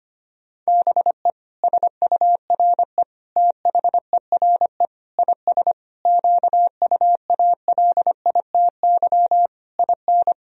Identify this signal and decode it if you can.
Morse code, heard as a keyed tone, about 25 wpm, 700 Hz. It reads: BE SURE THERE IS QUALITY IN
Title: Sentences from Top 1000 Words - Review Part 4 - 25wpm